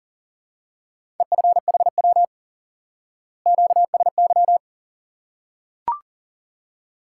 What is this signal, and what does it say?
Morse code, heard as a keyed tone, about 40 wpm, 700 Hz. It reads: EFHW QSY
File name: IQE Course 3 - Lesson 89 - YAGI - Two Elements - 40wpm